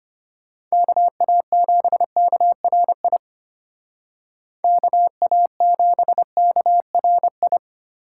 Morse code, heard as a keyed tone, about 32 wpm, 700 Hz. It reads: KA7KRI KA7KRI